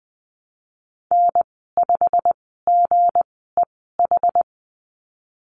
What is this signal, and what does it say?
Morse code, heard as a keyed tone, about 20 wpm, spaced as for 15 wpm, 700 Hz. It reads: N5GEH
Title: 238 Lesson 38g - Letter q - Call Signs 20wpm